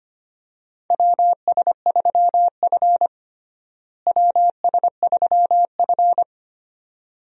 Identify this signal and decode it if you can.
Morse code, heard as a keyed tone, about 25 wpm, 700 Hz. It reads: WS3F WS3F
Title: USA Call Signs - 4 Characters - Repeated 2x - 25wpm